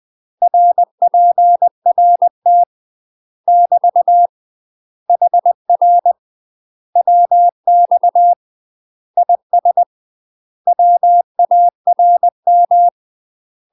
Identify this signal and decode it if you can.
Morse code, heard as a keyed tone, about 20 wpm, 700 Hz. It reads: RPRT <BT> HR WX IS WARM
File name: UK MASTER QSOs TRAINING / 20WPM / UK MASTER QSOs 0050